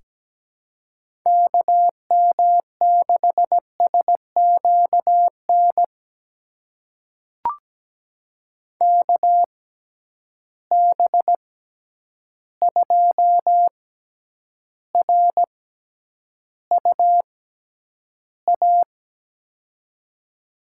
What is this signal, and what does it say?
Morse code, heard as a keyed tone, about 17 wpm, spaced as for 9 wpm, 700 Hz. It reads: KM6SQN KB2RUA
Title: USA Call Signs - 6 Characters - 6x Character Spacing - 17wpm